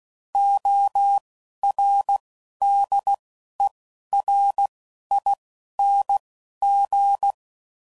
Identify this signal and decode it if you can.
Morse code, heard as a keyed tone, about 16 wpm, spaced as for 11 wpm, 800 Hz. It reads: ORDERING
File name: Prefixes / By First Letter / Os